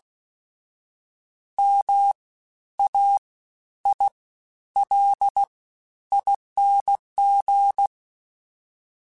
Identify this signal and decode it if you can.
Morse code, heard as a keyed tone, about 16 wpm, spaced as for 10 wpm, 800 Hz. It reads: MAILING
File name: Special INGs-mp3s / Special INGs-mp3s / Fast ing @15